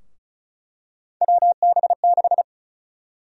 Morse code, heard as a keyed tone, about 36 wpm, 700 Hz. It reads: WB6
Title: USA Call Sign Prefixes - 35wpm